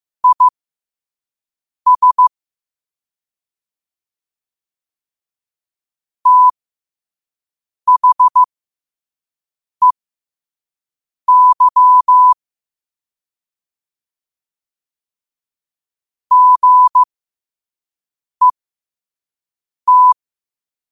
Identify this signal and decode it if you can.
Morse code, heard as a keyed tone, about 15 wpm, spaced as for 5 wpm, 1000 Hz. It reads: IS THEY GET
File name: qotdfiles / 5wpm